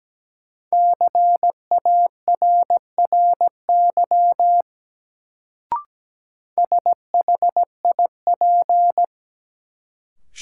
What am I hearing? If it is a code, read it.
Morse code, heard as a keyed tone, about 17 wpm, 700 Hz. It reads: CARRY SHIP
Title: Sentences from Top 400 Words - New Word Drill - 17wpm